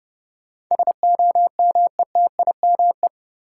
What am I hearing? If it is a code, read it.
Morse code, heard as a keyed tone, about 31 wpm, 700 Hz. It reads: SOMETIME
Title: Sentences from Top 1000 Words - Review Part 6 - New Word Drill - 30wpm